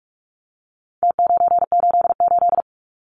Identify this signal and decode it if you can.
Morse code, heard as a keyed tone, about 44 wpm, 700 Hz. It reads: T988
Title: Numbers - Contest Style - 4-Digit 45wpm